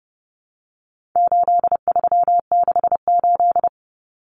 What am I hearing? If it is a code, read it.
Morse code, heard as a keyed tone, about 30 wpm, 700 Hz. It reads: 8368
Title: Numbers - 4-Digit 30wpm